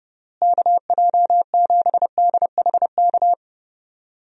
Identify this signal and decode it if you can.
Morse code, heard as a keyed tone, about 30 wpm, 700 Hz. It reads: KJ7DHK
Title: USA Call Signs - 6 Characters - Repeated 2x - 30wpm